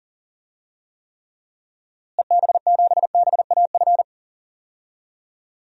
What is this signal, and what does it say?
Morse code, heard as a keyed tone, about 37 wpm, 700 Hz. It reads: EB7BAF
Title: DXCC-100 Call Signs - 6 Characters - Repeated 3x - 40wpm